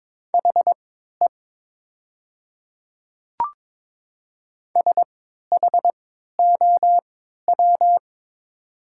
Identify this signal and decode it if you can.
Morse code, heard as a keyed tone, about 22 wpm, spaced as for 12 wpm, 700 Hz. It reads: HE SHOW
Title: 3x Character Spacing - Top 300 Words - 22wpm